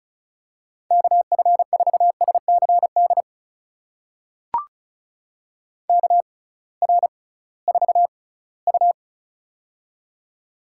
Morse code, heard as a keyed tone, about 36 wpm, spaced as for 20 wpm, 700 Hz. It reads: KF4SCD KR4U
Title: USA Call Signs - 6x Character Spacing - 35wpm